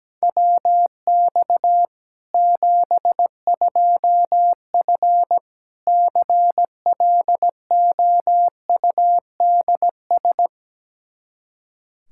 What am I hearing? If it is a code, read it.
Morse code, heard as a keyed tone, about 17 wpm, 700 Hz. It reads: WX 72F CLOUDS
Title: K1USN SST QSO Practice 17wpm